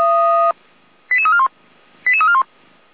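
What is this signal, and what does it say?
Five-tone selective call (ZVEI), heard as quick runs of five tones, tones of about 80 ms. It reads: 88341 88341
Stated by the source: ZVEI five-tone selective call 88291 received by SDR.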